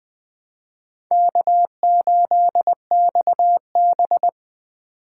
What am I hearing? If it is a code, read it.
Morse code, heard as a keyed tone, about 24 wpm, 700 Hz. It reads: K8XB
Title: Speed Racing - USA 1x2 Call Signs 20wpm